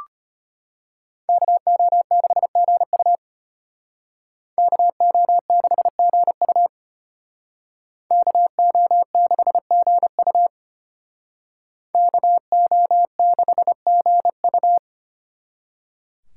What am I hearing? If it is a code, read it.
Morse code, heard as a keyed tone, about 33 wpm, 700 Hz. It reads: KO6GU KO6GU KO6GU KO6GU